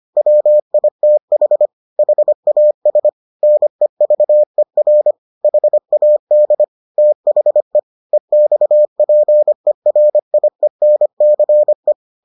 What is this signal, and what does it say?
Morse code, heard as a keyed tone, about 25 wpm, 600 Hz. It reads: WITH HAS NEVER HAD THE EXPERIENCE